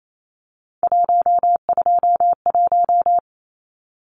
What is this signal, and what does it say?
Morse code, heard as a keyed tone, about 28 wpm, 700 Hz. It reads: 121
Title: Numbers - 3-Digit 28wpm